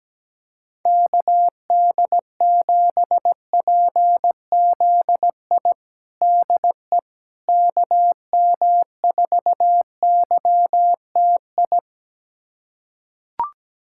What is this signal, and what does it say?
Morse code, heard as a keyed tone, about 17 wpm, 700 Hz. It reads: KD7PZI DE KM4YTI
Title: USA Callsign DE Callsign - 6 Characters - 17wpm